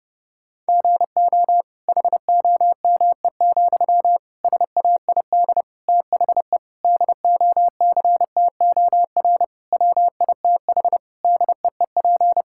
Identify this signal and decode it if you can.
Morse code, heard as a keyed tone, about 30 wpm, 700 Hz. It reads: GO HOME, SAID THE DOCTOR WITH DEEP
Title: Sentences from Top 1600 Words 30wpm